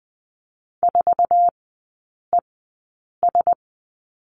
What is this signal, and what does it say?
Morse code, heard as a keyed tone, about 20 wpm, spaced as for 8 wpm, 700 Hz. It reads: 4ES
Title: 060 Lesson 11e - Letter h - Three Characters 20wpm